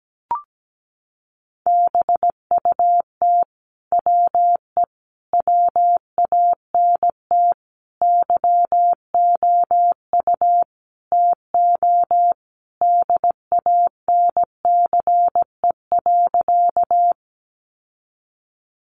Morse code, heard as a keyed tone, about 17 wpm, 700 Hz. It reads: BUT WE WANT YOU TO DANCE.
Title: Sentences from Top 700 Words 17wpm